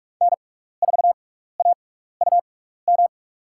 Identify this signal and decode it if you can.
Morse code, heard as a keyed tone, about 47 wpm, spaced as for 16 wpm, 700 Hz. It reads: N4AUM
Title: USA Call Signs - 5 Characters - 6x Character Spacing - 45wpm